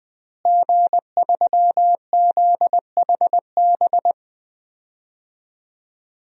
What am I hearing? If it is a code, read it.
Morse code, heard as a keyed tone, about 20 wpm, 700 Hz. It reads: G3ZHB
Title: DXCC-100 Call Signs - 5 Characters - Repeated 3x - 20wpm